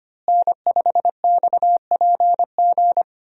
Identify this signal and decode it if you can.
Morse code, heard as a keyed tone, about 25 wpm, 700 Hz. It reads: N5XPG
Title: USA Call Signs - 5 Characters - Repeated 3x - 25wpm